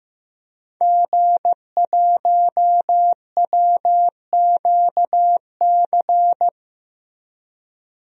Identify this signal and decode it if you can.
Morse code, heard as a keyed tone, about 15 wpm, 700 Hz. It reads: G1WQC
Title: DXCC-100 Call Signs - 5 Characters - Repeated 2x - 15wpm